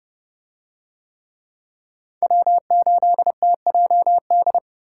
Morse code, heard as a keyed tone, about 30 wpm, 700 Hz. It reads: W8TJD